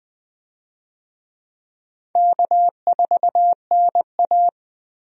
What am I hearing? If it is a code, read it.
Morse code, heard as a keyed tone, about 20 wpm, 700 Hz. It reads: K4NA
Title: USA Call Signs - 4 Characters - Repeated 2x - 20wpm